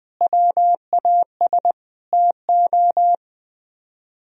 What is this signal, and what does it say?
Morse code, heard as a keyed tone, about 21 wpm, 700 Hz. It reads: WAS TO
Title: Speed-Racing - Sets of 5 Words 20wpm